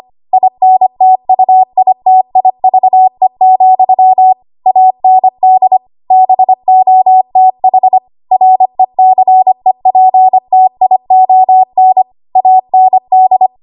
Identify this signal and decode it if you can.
Morse code, heard as a keyed tone, about 25 wpm, 750 Hz. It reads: INTUITIVE, AND BOTH RECEPTION AND
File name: Archive / 25 WPM / 25WPM